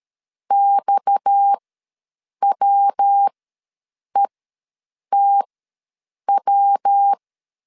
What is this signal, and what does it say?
Morse code, heard as a keyed tone, about 13 wpm, spaced as for 7 wpm, 800 Hz. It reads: XWETW
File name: LEZIONI / 7 GF9Z / 7 G ws 12wpm 3140